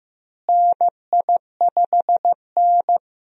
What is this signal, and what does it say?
Morse code, heard as a keyed tone, about 16 wpm, 700 Hz. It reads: NI5N
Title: USA Call Signs - 4 Characters - Repeated 2x - 15wpm